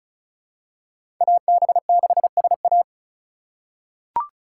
Morse code, heard as a keyed tone, about 36 wpm, 700 Hz. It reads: AB6SA
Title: USA Call Signs - 5 Characters - 2x Character Spacing - 35wpm